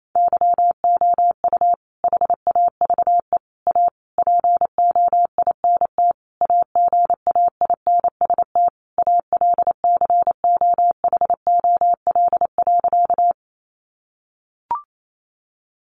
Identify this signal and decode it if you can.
Morse code, heard as a keyed tone, about 28 wpm, 700 Hz. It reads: YOU HAVE A POINT AGAINST ALCOHOL.
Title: Sentences from Top 1400 Words 28wpm